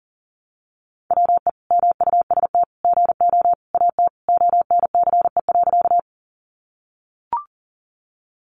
Morse code, heard as a keyed tone, about 40 wpm, 700 Hz. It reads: WE MUST GO AT ONCE.
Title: Sentences from Top 300 Words 40wpm